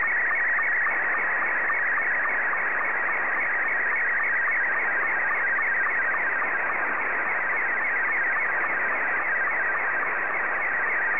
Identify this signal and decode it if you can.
SSTV (slow-scan television), Scottie 2.
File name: sstv / S2-VK